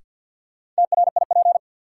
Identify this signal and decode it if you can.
Morse code, heard as a keyed tone, about 51 wpm, 700 Hz. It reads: TRIP
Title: Sentences from Top 1000 Words - Review Part 4 - New Word Drill - 50wpm